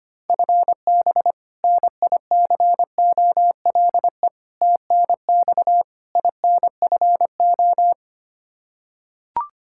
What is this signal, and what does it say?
Morse code, heard as a keyed tone, about 25 wpm, 700 Hz. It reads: FB NICOLE TNX INFO
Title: Practice QSO - Exchange 3 - More Info - 25wpm